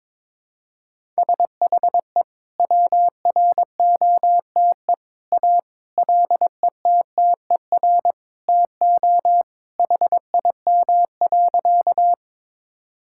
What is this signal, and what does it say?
Morse code, heard as a keyed tone, about 22 wpm, 700 Hz. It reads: SHE WROTE A LETTER TO HIM.
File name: practice / 22wpm / Sentences with 2-Word Phrases - 22wpm